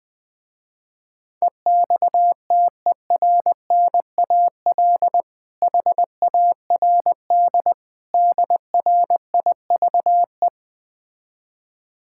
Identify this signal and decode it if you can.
Morse code, heard as a keyed tone, about 20 wpm, 700 Hz. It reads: EXTERNAL HARD DRIVE
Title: Difficult Words with Hints - 20wpm